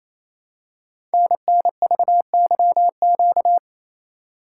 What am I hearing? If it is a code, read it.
Morse code, heard as a keyed tone, about 29 wpm, 700 Hz. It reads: NNVYQ
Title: Code Groups - 5 Letters - 28wpm